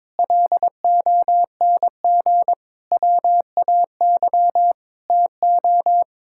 Morse code, heard as a keyed tone, about 22 wpm, 700 Hz. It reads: LONG WAY TO
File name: Sets of 9 Words - 22wpm